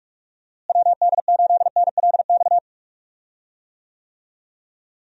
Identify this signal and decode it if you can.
Morse code, heard as a keyed tone, about 44 wpm, 700 Hz. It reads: WD8NLX
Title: USA Call Signs - 6 Characters - Repeated 2x - 45wpm